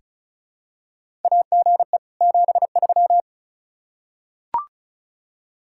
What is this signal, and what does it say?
Morse code, heard as a keyed tone, about 36 wpm, 700 Hz. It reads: AGE 73
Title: Practice QSO - Exchange 3 - More Info - 35wpm